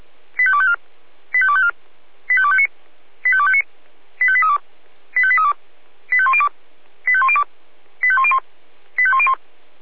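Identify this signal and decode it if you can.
Five-tone selective call (ZVEI), heard as quick runs of five tones, tones of about 70 ms. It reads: 85254 85254 85259 85259 85832 85832 85102 85102 85101 85101
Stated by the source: ZVEI five-tone call 85211 received by SDR.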